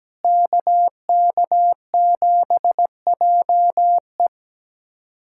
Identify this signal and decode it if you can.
Morse code, heard as a keyed tone, about 17 wpm, 700 Hz. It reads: KK7JE
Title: USA Callsign DE Callsign - 5 Characters - 17wpm